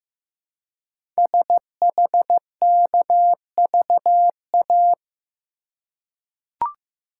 Morse code, heard as a keyed tone, about 15 wpm, 700 Hz. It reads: SHKVA
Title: Code Groups - 5 Letters - 15wpm